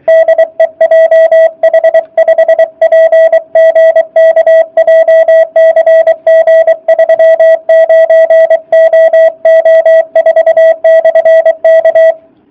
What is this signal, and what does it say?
Morse code, heard as a keyed tone, about 24 wpm, 660 Hz. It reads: DE JH5PGK JCG 39OO4/K